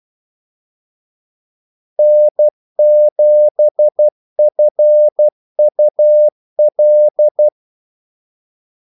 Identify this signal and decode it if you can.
Morse code, heard as a keyed tone, about 12 wpm, 600 Hz. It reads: N7FUL
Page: 2 ekstra ordmellemrum mellem kaldesignalerne